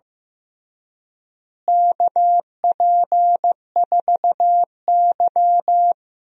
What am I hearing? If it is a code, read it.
Morse code, heard as a keyed tone, about 15 wpm, 700 Hz. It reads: KP4Y
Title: DXCC-100 Call Signs - 4 Characters - Repeated 3x - 15wpm